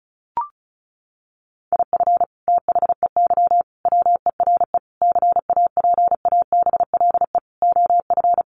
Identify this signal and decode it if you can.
Morse code, heard as a keyed tone, about 34 wpm, 700 Hz. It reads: IF THEY WERE CAPABLE OF
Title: Sentences from Top 1200 Words 35wpm